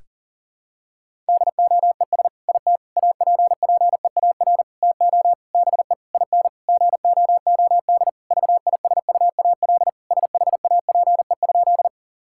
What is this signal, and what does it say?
Morse code, heard as a keyed tone, about 40 wpm, 700 Hz. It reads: DOES IT APPEAR TO BE IN GOOD VISUAL SHAPE?
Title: Sentences from Top 1000 Words - Review Part 6 - 40wpm